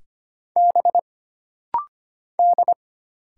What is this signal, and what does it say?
Morse code, heard as a keyed tone, about 25 wpm, 700 Hz. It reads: B D
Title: Single Letters - Rapid-Fire 25wpm